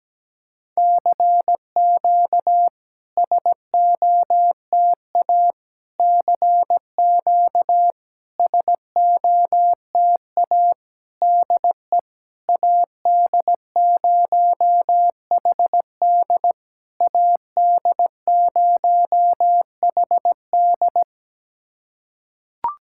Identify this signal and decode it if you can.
Morse code, heard as a keyed tone, about 17 wpm, 700 Hz. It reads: CQ SOTA CQ SOTA DE AD0HD AD0HD
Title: SOTA QSO Practice 17wpm